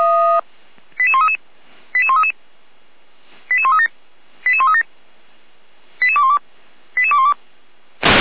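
Five-tone selective call (ZVEI), heard as quick runs of five tones, tones of about 70 ms. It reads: 88133 88133 88137 88137 88212 88212
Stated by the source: ZVEI five-tone selective call received by SDR, from the Feuerwehr Orlamünde fire brigade audio server.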